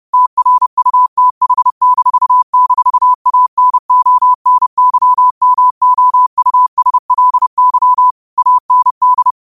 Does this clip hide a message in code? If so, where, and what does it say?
Yes: Morse code from 0.1 to 9.3 s, heard as a keyed tone, about 30 wpm, 1000 Hz. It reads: TRUTH--ANONYMOUSLY AND